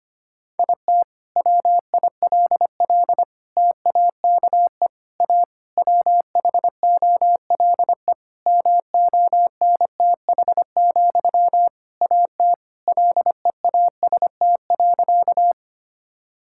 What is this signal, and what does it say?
Morse code, heard as a keyed tone, about 25 wpm, 700 Hz. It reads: IT WILL TAKE A WHOLE MONTH, AT LEAST.
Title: Sentences from Top 1000 Words - Review Part 5 - 25wpm